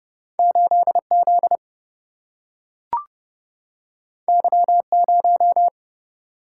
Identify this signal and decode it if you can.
Morse code, heard as a keyed tone, about 30 wpm, 700 Hz. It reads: 8 Z Y 0
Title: Single-Letter Single-Number 30wpm